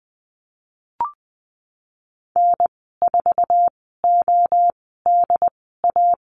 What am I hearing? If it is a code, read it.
Morse code, heard as a keyed tone, about 19 wpm, spaced as for 14 wpm, 700 Hz. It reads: N4ODA
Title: 076 Lesson 13g - Letter L - Call Signs 20wpm